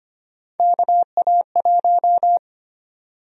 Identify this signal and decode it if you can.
Morse code, heard as a keyed tone, about 25 wpm, 700 Hz. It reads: KA1
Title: USA Call Sign Prefixes - 25wpm